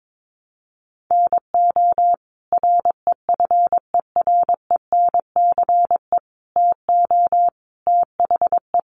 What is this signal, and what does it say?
Morse code, heard as a keyed tone, about 22 wpm, 700 Hz. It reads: NO REFERENCE TO THE
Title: Sentences from Top 1200 Words 22wpm